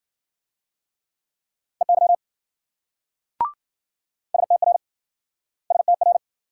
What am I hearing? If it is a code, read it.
Morse code, heard as a keyed tone, about 47 wpm, 700 Hz. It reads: EY STR STR
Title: Speed-Racing Multi-letter Phonemes - 40wpm